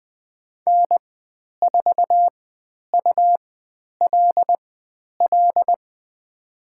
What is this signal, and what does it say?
Morse code, heard as a keyed tone, about 20 wpm, spaced as for 10 wpm, 700 Hz. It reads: N4ULL
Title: 102 Lesson 17g - Letter u - Call Signs 20wpm